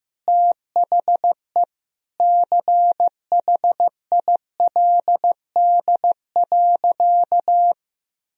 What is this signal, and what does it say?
Morse code, heard as a keyed tone, about 15 wpm, 700 Hz. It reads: THE CHILD.
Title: Sentences from Top 1000 Words - Review Part 1 - 15wpm